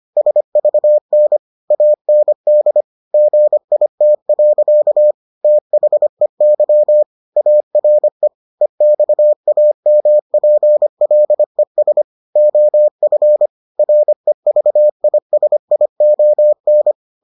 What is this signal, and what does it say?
Morse code, heard as a keyed tone, about 25 wpm, 600 Hz. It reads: SVN AND GIT. THEY ARE EXAMPLES OF REVISION